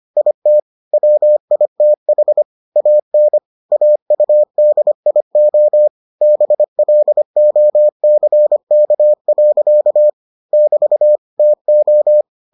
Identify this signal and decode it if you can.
Morse code, heard as a keyed tone, about 25 wpm, 600 Hz. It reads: IT WITH AN AUDIO BLOCK. <BT> TO